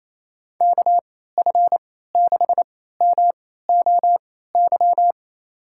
Morse code, heard as a keyed tone, about 27 wpm, spaced as for 16 wpm, 700 Hz. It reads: KF6MOY
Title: USA Call Signs - 6 Characters - 3x Character Spacing - 28wpm